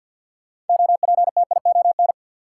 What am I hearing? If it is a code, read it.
Morse code, heard as a keyed tone, about 52 wpm, 700 Hz. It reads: OPTION
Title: Sentences from Top 1300 Words - New Word Drill - 50wpm